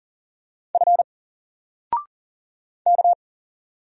Morse code, heard as a keyed tone, about 38 wpm, 700 Hz. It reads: F K
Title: Single Letter-Number Alt 40wpm